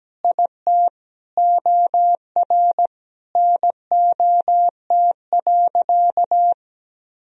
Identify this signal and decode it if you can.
Morse code, heard as a keyed tone, about 17 wpm, 700 Hz. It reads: IT OR NOT.
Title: Sentences from Top 1000 Words - Review Part 3 - 17wpm